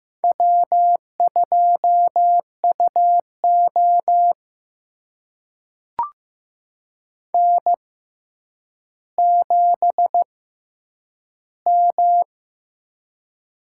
Morse code, heard as a keyed tone, about 15 wpm, spaced as for 9 wpm, 700 Hz. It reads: W2UO N7M
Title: USA Call Signs - 4 Characters - 6x Character Spacing - 15wpm